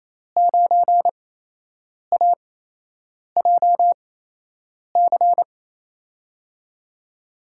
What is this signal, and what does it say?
Morse code, heard as a keyed tone, about 28 wpm, spaced as for 8 wpm, 700 Hz. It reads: 9AJC
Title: USA Call Signs - 5 Characters - 8x Character Spacing - 28wpm